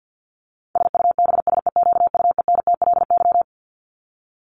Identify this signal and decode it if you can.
Morse code, heard as a keyed tone, about 50 wpm, 700 Hz. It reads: SUBSEQUENTLY
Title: LY Suffix - Top 100 Words with Base Words 50wpm